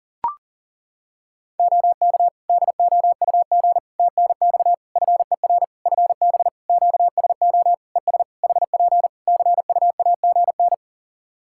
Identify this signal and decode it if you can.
Morse code, heard as a keyed tone, about 40 wpm, 700 Hz. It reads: OK DOUG TNX FER FB QSO ES HP CUAGN
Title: Practice QSO - Exchange 4 - Ending - 40wpm